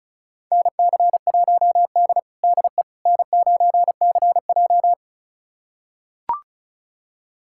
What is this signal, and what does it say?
Morse code, heard as a keyed tone, about 36 wpm, 700 Hz. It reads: NC1D DE N9CJ
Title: USA Callsign DE Callsign - 4 Characters - 35wpm